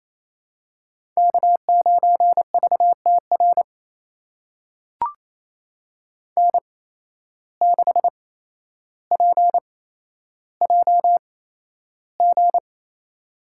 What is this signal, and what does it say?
Morse code, heard as a keyed tone, about 28 wpm, spaced as for 12 wpm, 700 Hz. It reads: K9VTR N6PJG
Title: USA Call Signs - 5 Characters - 8x Character Spacing - 28wpm